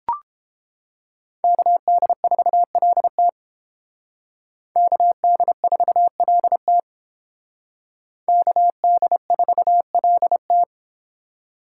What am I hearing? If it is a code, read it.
Morse code, heard as a keyed tone, about 30 wpm, 700 Hz. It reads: KD4LT KD4LT KD4LT